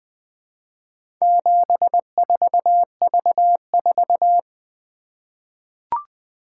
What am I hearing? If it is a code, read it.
Morse code, heard as a keyed tone, about 20 wpm, 700 Hz. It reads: 74V4
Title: Four Letter-Number Combinations - 20wpm